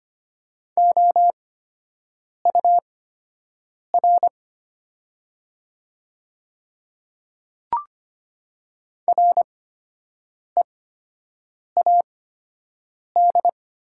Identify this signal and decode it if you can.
Morse code, heard as a keyed tone, about 25 wpm, spaced as for 7 wpm, 700 Hz. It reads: OUR READ
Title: 8x Character Spacing - Top 300 Words - 25wpm